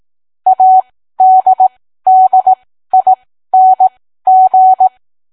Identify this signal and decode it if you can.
Morse code, heard as a keyed tone, about 18 wpm, spaced as for 13 wpm, 750 Hz. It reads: ADDING